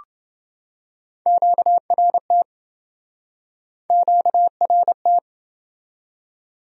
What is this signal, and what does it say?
Morse code, heard as a keyed tone, about 27 wpm, 700 Hz. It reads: QRT QRT